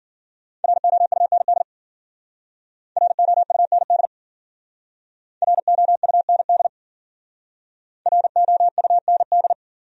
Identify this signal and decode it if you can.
Morse code, heard as a keyed tone, about 53 wpm, 700 Hz. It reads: ROUND ROUND ROUND ROUND